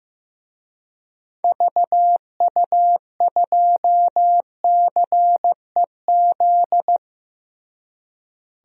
Morse code, heard as a keyed tone, about 15 wpm, 700 Hz. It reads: VU2CEZ
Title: DXCC-100 Call Signs - 6 Characters - Repeated 2x - 15wpm